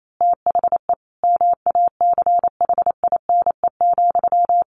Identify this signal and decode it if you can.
Morse code, heard as a keyed tone, about 28 wpm, 700 Hz. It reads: THE MACHINE,
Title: A Sound Of Thunder 28wpm